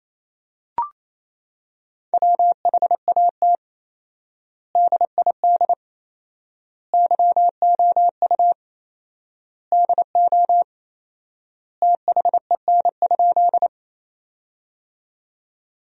Morse code, heard as a keyed tone, about 28 wpm, 700 Hz. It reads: WHAT DID YOU DO THEN?
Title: 4x Word-Spacing - Sentences from Top 100 Words - 28wpm